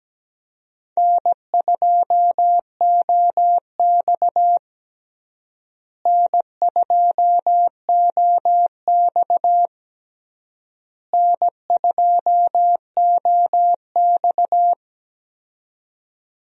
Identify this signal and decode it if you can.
Morse code, heard as a keyed tone, about 17 wpm, 700 Hz. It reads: N2OX N2OX N2OX
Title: USA Call Signs - 4 Characters - Repeated 3x - 17wpm